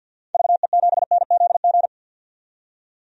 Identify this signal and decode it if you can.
Morse code, heard as a keyed tone, about 52 wpm, 700 Hz. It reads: VE7NZG
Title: DXCC-100 Call Signs - 6 Characters - 50wpm